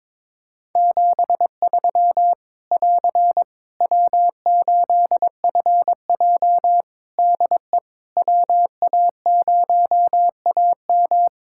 Morse code, heard as a keyed tone, about 22 wpm, 700 Hz. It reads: 73 <AR> W8FJ DE WA0AM
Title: Practice QSO - Exchange 4 - Ending - 22wpm